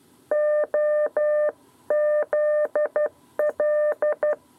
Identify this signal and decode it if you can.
Morse code, heard as a keyed tone, about 12 wpm, 600 Hz. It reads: OZL
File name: 7b lezione cw 2